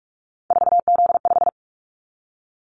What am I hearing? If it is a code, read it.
Morse code, heard as a keyed tone, about 46 wpm, 700 Hz. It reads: 4Z5
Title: Callsign Prefixes for Top 50 DXCC - 45wpm